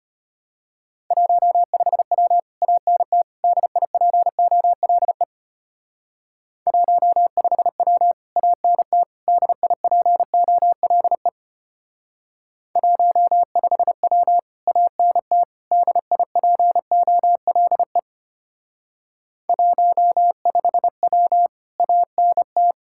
Morse code, heard as a keyed tone, about 33 wpm, 700 Hz. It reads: 15W ANT DIPOLE 15W ANT DIPOLE 15W ANT DIPOLE 15W ANT